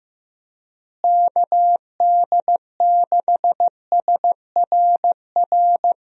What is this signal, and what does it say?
Morse code, heard as a keyed tone, about 15 wpm, 700 Hz. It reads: KD6SRR
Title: USA Call Signs - 6 Characters - Repeated 3x - 15wpm